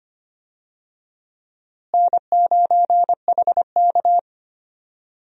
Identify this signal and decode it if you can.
Morse code, heard as a keyed tone, about 25 wpm, 700 Hz. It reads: N9HK
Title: USA Call Signs - 4 Characters - Repeated 3x - 25wpm